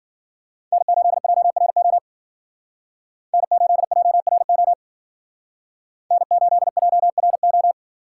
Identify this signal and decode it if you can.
Morse code, heard as a keyed tone, about 52 wpm, 700 Hz. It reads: N8JRO N8JRO N8JRO